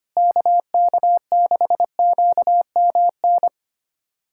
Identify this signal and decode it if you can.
Morse code, heard as a keyed tone, about 25 wpm, 700 Hz. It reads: KK6QMN
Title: USA Callsign DE Callsign 25wpm